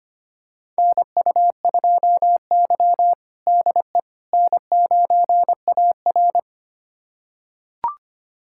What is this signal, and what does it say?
Morse code, heard as a keyed tone, about 25 wpm, 700 Hz. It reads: NU2Y DE N9AR
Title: USA Callsign DE Callsign - 4 Characters - 25wpm